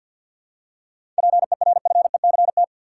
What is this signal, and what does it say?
Morse code, heard as a keyed tone, about 53 wpm, 700 Hz. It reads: PERFECT
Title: Seven Letter Words - 50wpm